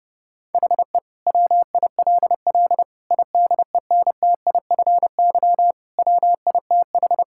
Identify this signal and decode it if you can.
Morse code, heard as a keyed tone, about 29 wpm, 700 Hz. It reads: HE WILL IDENTIFY WITH
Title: Sentences from Top 1000 Words - Review Part 6 - 30wpm